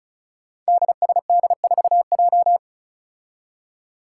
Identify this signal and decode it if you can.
Morse code, heard as a keyed tone, about 36 wpm, 700 Hz. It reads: DSD4J
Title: Five Challenging Letter-Number Combinations 35wpm